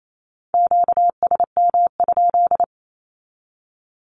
Morse code, heard as a keyed tone, about 28 wpm, 700 Hz. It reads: QSM?
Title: Q-Codes 28wpm